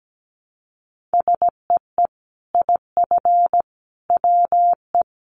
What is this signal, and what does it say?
Morse code, heard as a keyed tone, about 17 wpm, 700 Hz. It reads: SEE IF WE
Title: Sets of 5 Words Encore2 17wpm